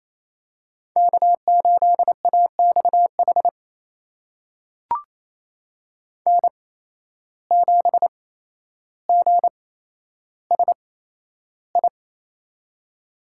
Morse code, heard as a keyed tone, about 28 wpm, spaced as for 12 wpm, 700 Hz. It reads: K8AXH N7GSI
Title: USA Call Signs - 5 Characters - 8x Character Spacing - 28wpm